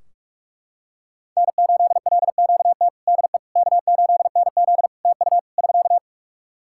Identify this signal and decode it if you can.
Morse code, heard as a keyed tone, about 44 wpm, 700 Hz. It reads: N8LQT DE K8NZ TU <SK>
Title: Practice QSO - Exchange 4 - Ending - 45wpm